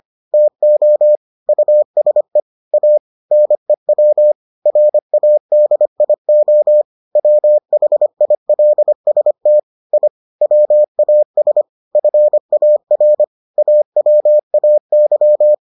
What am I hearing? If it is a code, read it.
Morse code, heard as a keyed tone, about 25 wpm, 600 Hz. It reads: TO USE A NEW RADIO WHILST I WAS FAR AWAY